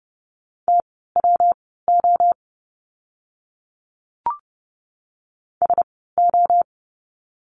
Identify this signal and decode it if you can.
Morse code, heard as a keyed tone, about 30 wpm, spaced as for 17 wpm, 700 Hz. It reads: TWO SO
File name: course / 30:10wpm / 230 Lesson 37f - Letter x - Words 30wpm